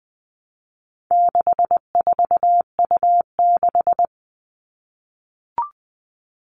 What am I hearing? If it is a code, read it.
Morse code, heard as a keyed tone, about 20 wpm, 700 Hz. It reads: 64U6
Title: All Cut Numbers - 4-Digit 20wpm